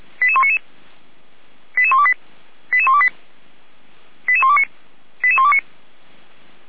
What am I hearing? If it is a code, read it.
Five-tone selective call (ZVEI), heard as quick runs of five tones, tones of about 80 ms. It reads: 88199 88128 88128 88129 88129
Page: ZVEI